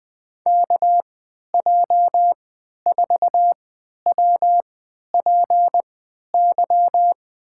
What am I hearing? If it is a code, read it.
Morse code, heard as a keyed tone, about 20 wpm, spaced as for 11 wpm, 700 Hz. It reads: KJ4WPY
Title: USA Call Signs - 6 Characters - 3x Character Spacing - 20wpm